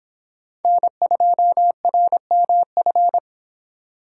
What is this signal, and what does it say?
Morse code, heard as a keyed tone, about 26 wpm, 700 Hz. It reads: N2RMF